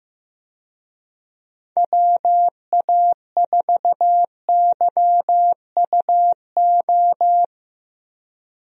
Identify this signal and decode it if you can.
Morse code, heard as a keyed tone, about 15 wpm, 700 Hz. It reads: WA4YUO
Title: USA Call Signs - 6 Characters - Repeated 2x - 15wpm